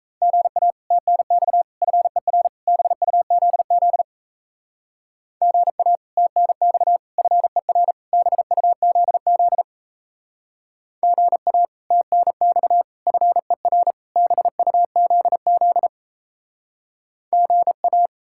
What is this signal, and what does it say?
Morse code, heard as a keyed tone, about 34 wpm, 700 Hz. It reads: GA TNX FER BUZZ GA TNX FER BUZZ GA TNX FER BUZZ GA
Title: Speed-Racing Common QSO Phrases - 28wpm